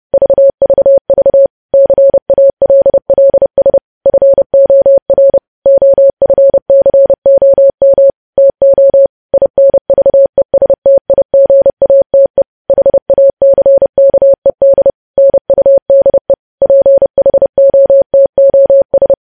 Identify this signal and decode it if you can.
Morse code, heard as a keyed tone, about 30 wpm, 550 Hz. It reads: VVV CALLS FOR OFCOM TO INVESTIGATE HACKED NUDE PHOTOS
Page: news headlines in morse code at 30 wpm